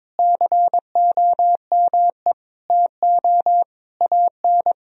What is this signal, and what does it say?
Morse code, heard as a keyed tone, about 21 wpm, 700 Hz. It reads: COME TO AN
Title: Sentences from Top 1000 Words - Review Part 3 - 22wpm